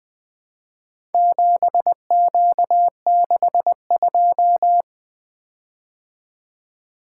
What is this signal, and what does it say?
Morse code, heard as a keyed tone, about 20 wpm, 700 Hz. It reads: 7Q62
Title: Four Letter-Number Combinations - 20wpm